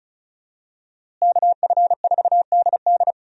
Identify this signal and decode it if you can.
Morse code, heard as a keyed tone, about 36 wpm, 700 Hz. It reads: KF4DD
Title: USA Call Signs - 5 Characters - Repeated 3x - 35wpm